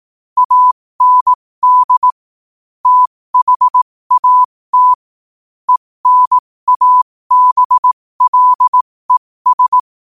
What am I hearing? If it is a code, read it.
Morse code, heard as a keyed tone, about 18 wpm, spaced as for 15 wpm, 1000 Hz. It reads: AND THAT ENABLES
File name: 15wpm